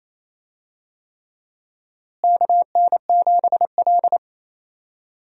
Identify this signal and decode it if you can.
Morse code, heard as a keyed tone, about 28 wpm, 700 Hz. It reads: KN7L
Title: USA Call Signs - 4 Characters - Repeated 3x - 28wpm